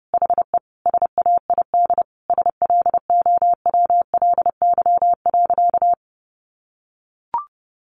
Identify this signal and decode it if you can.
Morse code, heard as a keyed tone, about 28 wpm, 700 Hz. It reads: HE SAID SLOWLY.
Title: Sentences from Top 1100 Words 30wpm